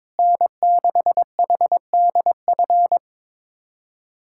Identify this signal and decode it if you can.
Morse code, heard as a keyed tone, about 22 wpm, 700 Hz. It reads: N6HDF
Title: USA Callsign DE Callsign 22wpm